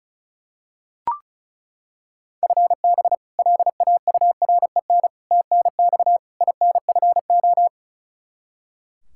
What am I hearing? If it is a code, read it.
Morse code, heard as a keyed tone, about 36 wpm, 700 Hz. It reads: FB LAUREN TNX INFO
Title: K1USN SST QSO Practice 35wpm